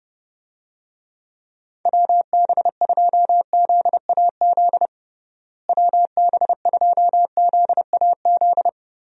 Morse code, heard as a keyed tone, about 32 wpm, 700 Hz. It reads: WB2ZAZ WB2ZAZ
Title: USA Call Signs - 6 Characters - Repeated 3x - 30wpm